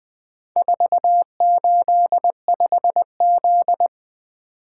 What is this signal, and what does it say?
Morse code, heard as a keyed tone, about 20 wpm, 700 Hz. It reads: 485Z
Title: Four Letter-Number Combinations - 20wpm